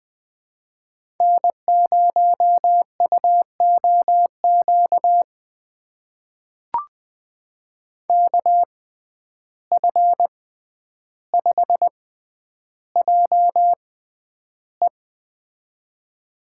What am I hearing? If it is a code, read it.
Morse code, heard as a keyed tone, about 20 wpm, spaced as for 10 wpm, 700 Hz. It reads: N0UOQ KF5JE
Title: USA Call Signs - 6x Character Spacing - 20wpm